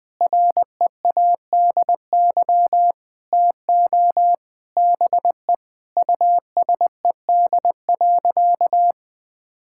Morse code, heard as a keyed tone, about 21 wpm, 700 Hz. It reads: READY TO BE USED.
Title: Sentences from Top 1600 Words 20wpm